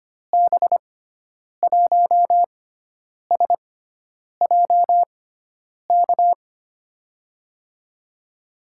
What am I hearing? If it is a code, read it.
Morse code, heard as a keyed tone, about 25 wpm, spaced as for 9 wpm, 700 Hz. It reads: B1SJK
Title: USA Call Signs - 6 Characters - 6x Character Spacing - 25wpm